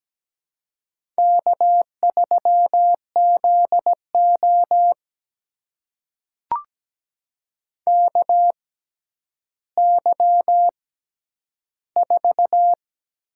Morse code, heard as a keyed tone, about 17 wpm, spaced as for 10 wpm, 700 Hz. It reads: K3ZO KY4
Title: USA Call Signs - 4 Characters - 6x Character Spacing - 17wpm